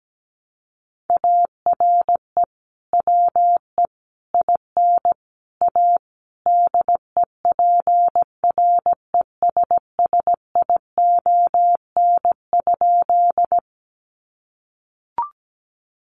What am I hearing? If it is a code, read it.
Morse code, heard as a keyed tone, about 17 wpm, 700 Hz. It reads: ARE WE IN A DEPRESSION?
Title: Sentences from Top 1300 Words 17wpm